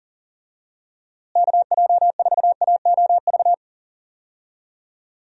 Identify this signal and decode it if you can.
Morse code, heard as a keyed tone, about 40 wpm, 700 Hz. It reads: KJ4AOV
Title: USA Call Signs - 6 Characters - Repeated 2x - 40wpm